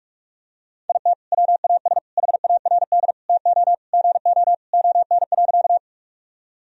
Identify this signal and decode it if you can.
Morse code, heard as a keyed tone, about 42 wpm, 700 Hz. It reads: IT WAS HARD TO GO ON.
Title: Sentences from Top 1000 Words - Review Part 1 - 45wpm